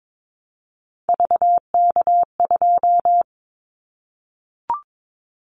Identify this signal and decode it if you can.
Morse code, heard as a keyed tone, about 22 wpm, 700 Hz. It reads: VK2
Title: Callsign Prefixes for Top 50 DXCC - 22wpm